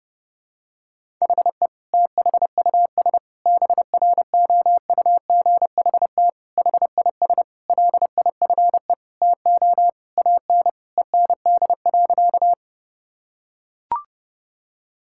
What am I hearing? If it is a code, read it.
Morse code, heard as a keyed tone, about 30 wpm, 700 Hz. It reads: HE THUS BROUGHT HIS LIFE TO AN END.
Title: Sentences from Top 1000 Words - Review Part 5 - 30wpm